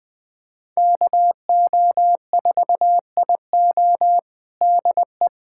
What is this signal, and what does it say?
Morse code, heard as a keyed tone, about 20 wpm, 700 Hz. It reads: KO4IO DE
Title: USA Callsign DE Callsign 20wpm